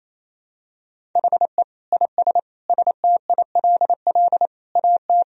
Morse code, heard as a keyed tone, about 28 wpm, 700 Hz. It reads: HE IS STILL AT
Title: Sentences from Top 1000 Words - Review Part 1 - 28wpm